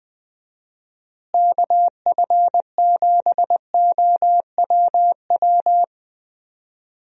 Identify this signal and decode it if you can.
Morse code, heard as a keyed tone, about 20 wpm, 700 Hz. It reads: KF7OWW
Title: USA Call Signs - 6 Characters - Repeated 3x - 20wpm